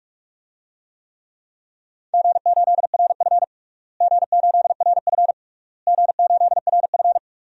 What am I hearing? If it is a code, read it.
Morse code, heard as a keyed tone, about 46 wpm, 700 Hz. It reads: G8RF G8RF G8RF
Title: DXCC-100 Call Signs - 4 Characters - Repeated 3x - 45wpm